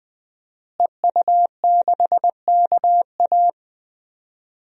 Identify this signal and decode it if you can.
Morse code, heard as a keyed tone, about 20 wpm, 700 Hz. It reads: EU6KA
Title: DXCC-100 Call Signs - 5 Characters - Repeated 2x - 20wpm